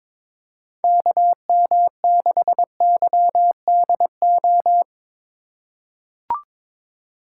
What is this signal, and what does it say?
Morse code, heard as a keyed tone, about 22 wpm, 700 Hz. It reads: KM6YDO
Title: USA Call Signs - 6 Characters - 2x Character Spacing - 22wpm